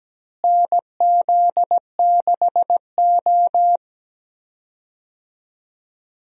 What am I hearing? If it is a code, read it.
Morse code, heard as a keyed tone, about 17 wpm, 700 Hz. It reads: NZ6O
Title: USA Call Signs - 4 Characters - Repeated 3x - 17wpm